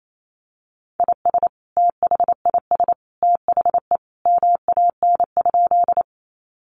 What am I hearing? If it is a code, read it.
Morse code, heard as a keyed tone, about 28 wpm, 700 Hz. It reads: IS THIS THE MAN?
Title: Sentences from Top 200 Words 28wpm